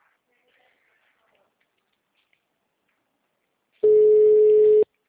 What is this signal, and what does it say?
Ringback tone (Europe, 425 Hz).